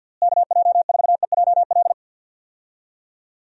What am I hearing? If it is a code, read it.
Morse code, heard as a keyed tone, about 52 wpm, 700 Hz. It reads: KJ4EJL
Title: USA Call Signs - 6 Characters - Repeated 2x - 50wpm